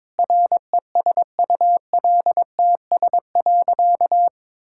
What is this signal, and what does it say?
Morse code, heard as a keyed tone, about 22 wpm, 700 Hz. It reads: RESULTS.
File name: Sentences with 7-Word Phrases - 22wpm